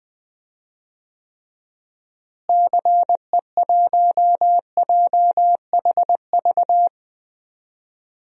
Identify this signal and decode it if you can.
Morse code, heard as a keyed tone, about 20 wpm, 700 Hz. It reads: CE1JHV
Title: DXCC-100 Call Signs - 6 Characters - Repeated 3x - 20wpm